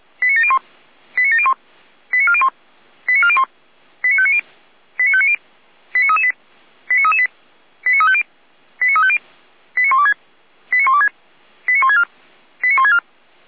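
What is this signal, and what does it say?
Five-tone selective call (ZVEI), heard as quick runs of five tones, tones of about 70 ms. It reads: 89701 89701 89401 89401 89590 89590 89338 89338 89350 89350 89126 89126 89164 89164
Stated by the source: Five-tone ZVEI selective call received on SDR.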